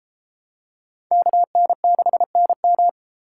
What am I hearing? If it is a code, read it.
Morse code, heard as a keyed tone, about 31 wpm, 700 Hz. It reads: KN6NM